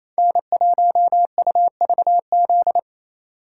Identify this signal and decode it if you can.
Morse code, heard as a keyed tone, about 28 wpm, 700 Hz. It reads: N1UVZ
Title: USA Call Signs - 2x Character Spacing - 28wpm